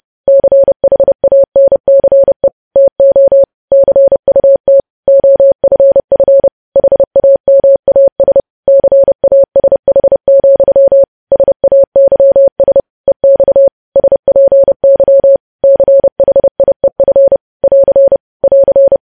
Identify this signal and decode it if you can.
Morse code, heard as a keyed tone, about 30 wpm, 550 Hz. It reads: CHANCE TO CUT OFF HAMAS CASH, SAYS EX SPY CHIEF <AR> <AR>
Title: News Headlines at 30